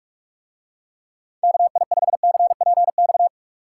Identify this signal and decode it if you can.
Morse code, heard as a keyed tone, about 44 wpm, 700 Hz. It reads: KI5CPX